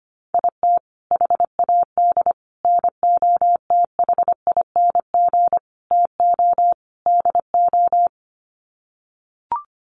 Morse code, heard as a keyed tone, about 25 wpm, 700 Hz. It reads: IT HAD NOTHING TO DO
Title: Sets of 5 Words Encore2 25wpm